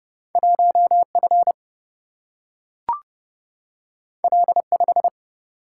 Morse code, heard as a keyed tone, about 32 wpm, 700 Hz. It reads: 1 F L 5
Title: Single-Letter Single-Number 30wpm